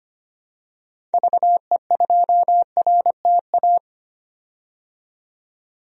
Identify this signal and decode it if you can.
Morse code, heard as a keyed tone, about 25 wpm, 700 Hz. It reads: VE2RTA
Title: DXCC-100 Call Signs - 6 Characters - 25wpm